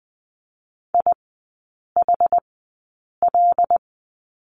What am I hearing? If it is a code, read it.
Morse code, heard as a keyed tone, about 20 wpm, spaced as for 8 wpm, 700 Hz. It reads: IHL
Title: 127 Lesson 21e - Number 6 - Three Characters 20wpm